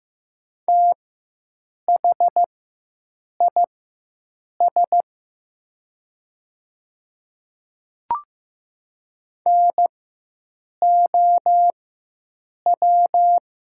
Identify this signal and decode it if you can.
Morse code, heard as a keyed tone, about 15 wpm, spaced as for 7 wpm, 700 Hz. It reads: THIS NOW
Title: 4x Character Spacing - Top 200 Words - 15wpm